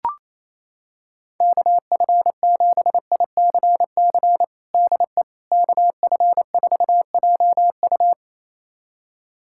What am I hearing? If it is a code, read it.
Morse code, heard as a keyed tone, about 28 wpm, 700 Hz. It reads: KF7ICC DE KF4JU
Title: USA Callsign DE Callsign 28wpm